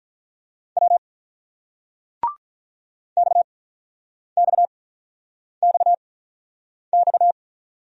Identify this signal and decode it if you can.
Morse code, heard as a keyed tone, about 48 wpm, spaced as for 9 wpm, 700 Hz. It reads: W XXXX